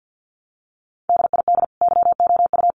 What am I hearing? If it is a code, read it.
Morse code, heard as a keyed tone, about 50 wpm, 700 Hz. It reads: DID YOU
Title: Sentences from Top 1100 Words 50wpm